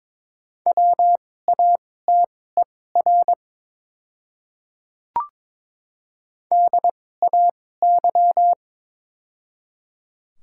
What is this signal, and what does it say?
Morse code, heard as a keyed tone, about 22 wpm, spaced as for 16 wpm, 700 Hz. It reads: WATER DAY
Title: 2x Character Spacing - Top 200 Words - 22wpm